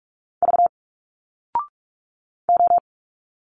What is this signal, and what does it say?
Morse code, heard as a keyed tone, about 46 wpm, 700 Hz. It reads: V O
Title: Single Letters 45wpm